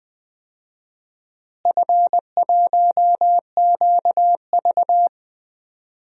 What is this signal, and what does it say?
Morse code, heard as a keyed tone, about 20 wpm, 700 Hz. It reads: F1QV
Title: DXCC-100 Call Signs - 4 Characters - Repeated 3x - 20wpm